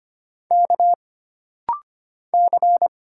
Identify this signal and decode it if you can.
Morse code, heard as a keyed tone, about 25 wpm, 700 Hz. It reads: K C
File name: Single Character (Letter:Number:Punctuation:Prosign) - Rapid-Fire - 25wpm